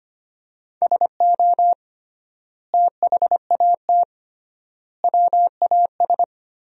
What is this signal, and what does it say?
Morse code, heard as a keyed tone, about 25 wpm, 700 Hz. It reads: SO THAT WAS
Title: 3x Word-Spacing - Sentences from Top 100 Words - 25wpm